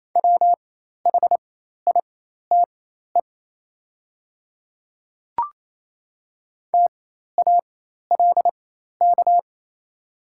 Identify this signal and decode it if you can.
Morse code, heard as a keyed tone, about 28 wpm, spaced as for 13 wpm, 700 Hz. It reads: WHITE TALK
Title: 4x Character Spacing - Top 500 Words - 28wpm